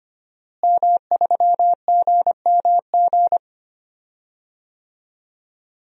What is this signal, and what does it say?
Morse code, heard as a keyed tone, about 25 wpm, 700 Hz. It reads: M3GMG
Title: DXCC-100 Call Signs - 5 Characters - Repeated 3x - 25wpm